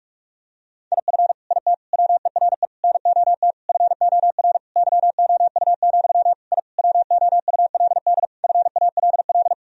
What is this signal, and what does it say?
Morse code, heard as a keyed tone, about 46 wpm, 700 Hz. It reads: IF IT WERE NOT FOR YOU, I WOULD FALL
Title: Sentences from Top 1000 Words - Review Part 2 - 45wpm